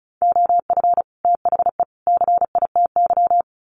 Morse code, heard as a keyed tone, about 34 wpm, 700 Hz. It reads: OF THE CITY
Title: Sets of 3 Words 35wpm